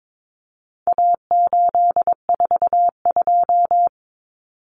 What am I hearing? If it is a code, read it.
Morse code, heard as a keyed tone, about 22 wpm, 700 Hz. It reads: A842
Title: Common Cut Numbers - 4-Digit 22wpm